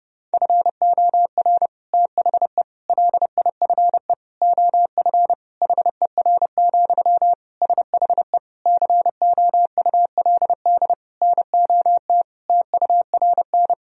Morse code, heard as a keyed tone, about 30 wpm, 700 Hz. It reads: FOR THE LIFE OF HER, SHE COULD NOT TURN
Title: Sentences from Top 1000 Words - Review Part 2 - 30wpm